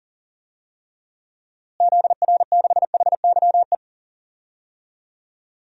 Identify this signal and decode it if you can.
Morse code, heard as a keyed tone, about 40 wpm, 700 Hz. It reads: ZR6HYE
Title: DXCC-100 Call Signs - 6 Characters - Repeated 2x - 40wpm